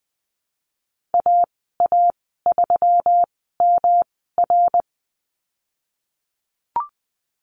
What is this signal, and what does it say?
Morse code, heard as a keyed tone, about 20 wpm, spaced as for 15 wpm, 700 Hz. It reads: AA3MR
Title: 128 Lesson 21g - Number 6 - Call Signs 20wpm